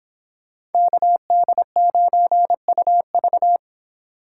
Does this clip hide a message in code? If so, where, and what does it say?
Yes: Morse code from 0.7 to 3.6 s, heard as a keyed tone, about 26 wpm, 700 Hz. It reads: KD9UV